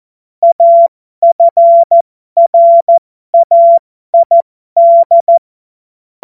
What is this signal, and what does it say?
Morse code, heard as a keyed tone, about 14 wpm, spaced as for 12 wpm, 680 Hz. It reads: AFRAID
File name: Morsekurs / Witze